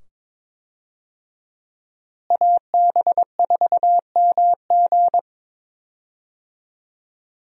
Morse code, heard as a keyed tone, about 22 wpm, 700 Hz. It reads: AB4MG
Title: USA Call Signs - 5 Characters - Repeated 2x - 22wpm